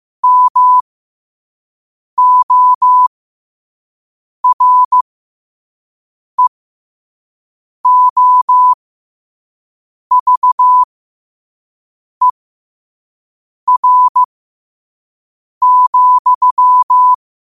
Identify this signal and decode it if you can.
Morse code, heard as a keyed tone, about 15 wpm, spaced as for 5 wpm, 1000 Hz. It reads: MOREOVER,
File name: qotdfiles / 5wpm